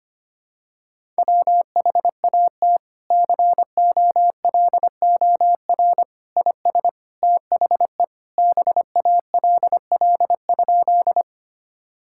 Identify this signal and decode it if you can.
Morse code, heard as a keyed tone, about 25 wpm, 700 Hz. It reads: WHAT COLOR IS THE BALL?
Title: Sentences from Top 1000 Words - Review Part 6 - 25wpm